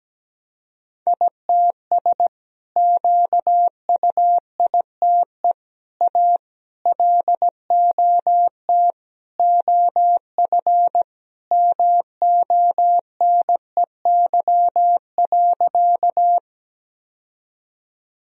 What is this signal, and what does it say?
Morse code, heard as a keyed tone, about 17 wpm, 700 Hz. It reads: ITS QUITE A LOT OF MONEY.
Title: Sentences from Top 1000 Words - Review Part 3 - 17wpm